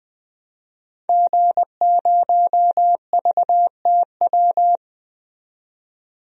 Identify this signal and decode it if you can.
Morse code, heard as a keyed tone, about 20 wpm, 700 Hz. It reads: G0VTW
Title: DXCC-100 Call Signs - 5 Characters - 20wpm